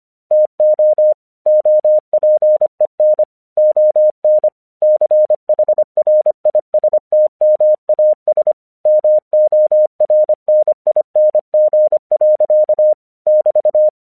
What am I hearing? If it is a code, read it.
Morse code, heard as a keyed tone, about 25 wpm, 610 Hz. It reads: TO OPEN ON CHRISTMAS MORNING. <BT>
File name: 1 English 25wpm / AQC17